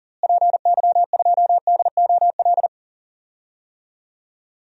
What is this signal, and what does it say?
Morse code, heard as a keyed tone, about 40 wpm, 700 Hz. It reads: PY2DOL
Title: DXCC-100 Call Signs - 6 Characters - Repeated 2x - 40wpm